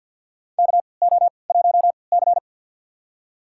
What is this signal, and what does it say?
Morse code, heard as a keyed tone, about 53 wpm, spaced as for 29 wpm, 700 Hz. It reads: KO1C
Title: USA Call Signs - 4 Characters - 3x Character Spacing - 50wpm